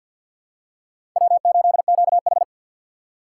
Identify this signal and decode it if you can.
Morse code, heard as a keyed tone, about 51 wpm, 700 Hz. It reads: W8QH
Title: USA Call Signs - 4 Characters - 3x Character Spacing - 50wpm